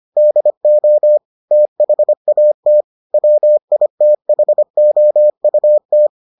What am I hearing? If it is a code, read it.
Morse code, heard as a keyed tone, about 25 wpm, 600 Hz. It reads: DO THAT WITHOUT